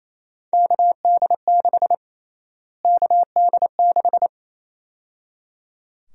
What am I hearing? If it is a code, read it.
Morse code, heard as a keyed tone, about 28 wpm, 700 Hz. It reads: KD6 KD6
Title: USA Call Sign Prefixes - Repeated 2x - 28wpm